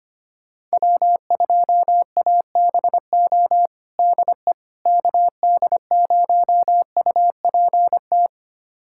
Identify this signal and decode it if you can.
Morse code, heard as a keyed tone, about 25 wpm, 700 Hz. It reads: W2ABO DE KD0UPT
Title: USA Callsign DE Callsign 25wpm